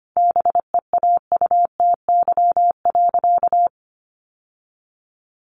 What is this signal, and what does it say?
Morse code, heard as a keyed tone, about 25 wpm, 700 Hz. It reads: BEAUTY.
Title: Sentences from Top 1200 Words 25wpm